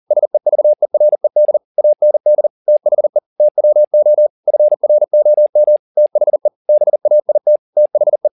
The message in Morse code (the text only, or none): SEVERED AND THE TWO FROM THE BAIT THE